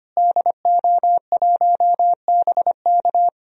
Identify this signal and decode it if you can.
Morse code, heard as a keyed tone, about 25 wpm, 700 Hz. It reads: DO1BK
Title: DXCC-100 Call Signs - 5 Characters - Repeated 3x - 25wpm